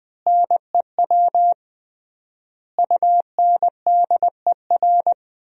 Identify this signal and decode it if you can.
Morse code, heard as a keyed tone, about 20 wpm, 700 Hz. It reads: NEW UNDER
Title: 3x Word-Spacing - Sentences from Top 500 Words - 20wpm